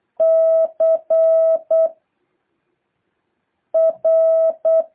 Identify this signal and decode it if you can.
Morse code, heard as a keyed tone, about 8 wpm, 650 Hz. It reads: C R
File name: jh8cbh / 2023cw